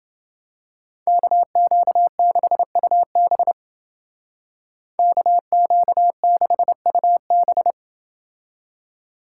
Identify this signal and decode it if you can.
Morse code, heard as a keyed tone, about 28 wpm, 700 Hz. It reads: KQ6UB KQ6UB